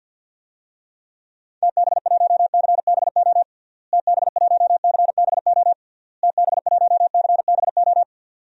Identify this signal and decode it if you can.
Morse code, heard as a keyed tone, about 50 wpm, 700 Hz. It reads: TB1CBO TB1CBO TB1CBO
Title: DXCC-100 Call Signs - 6 Characters - Repeated 3x - 50wpm